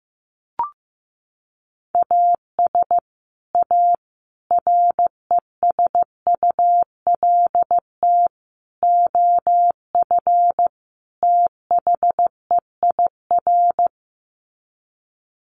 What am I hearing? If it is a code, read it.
Morse code, heard as a keyed tone, about 15 wpm, 700 Hz. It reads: AS A RESULT OF THEIR
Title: Sets of 5 Words Encore2 15wpm